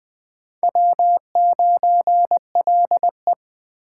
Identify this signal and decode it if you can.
Morse code, heard as a keyed tone, about 20 wpm, 700 Hz. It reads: W9LE
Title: Speed Racing - USA 1x2 Call Signs 17wpm